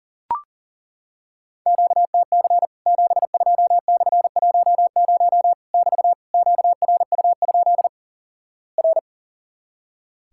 Morse code, heard as a keyed tone, about 37 wpm, 700 Hz. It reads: QTC 72/10 <BT> QRU? R
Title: Worked All Europe DX Contest - QTC Exchange - 40wpm